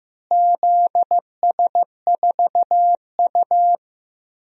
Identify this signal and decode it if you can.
Morse code, heard as a keyed tone, about 15 wpm, 700 Hz. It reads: ZS4U
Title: DXCC-100 Call Signs - 4 Characters - Repeated 2x - 15wpm